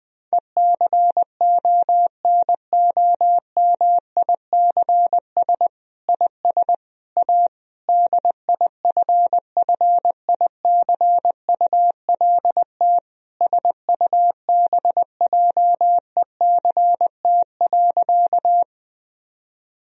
Morse code, heard as a keyed tone, about 20 wpm, 700 Hz. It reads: ECONOMICS IS A DIFFICULT SUBJECT.
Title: Sentences from Top 1000 Words - Review Part 3 - 20wpm